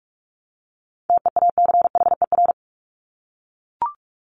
Morse code, heard as a keyed tone, about 44 wpm, 700 Hz. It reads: TEACHER
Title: Top 751-1000 Words 45wpm